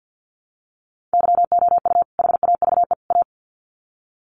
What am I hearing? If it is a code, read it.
Morse code, heard as a keyed tone, about 50 wpm, 700 Hz. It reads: YOU HAVE A
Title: Sets of 3 Words 50wpm